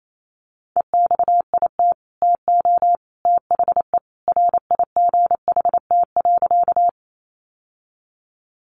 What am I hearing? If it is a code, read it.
Morse code, heard as a keyed tone, about 28 wpm, 700 Hz. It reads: EXIT TO THE RIGHT.
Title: Sentences from Top 1100 Words 28wpm